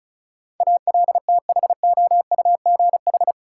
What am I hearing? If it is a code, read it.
Morse code, heard as a keyed tone, about 36 wpm, 700 Hz. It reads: ALTHOUGH
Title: Sentences from Top 600 Words - New Word Drill - 35wpm